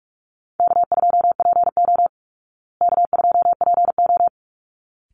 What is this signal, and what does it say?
Morse code, heard as a keyed tone, about 44 wpm, 700 Hz. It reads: K2PO K2PO
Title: Call Signs Repeated 3x - 45wpm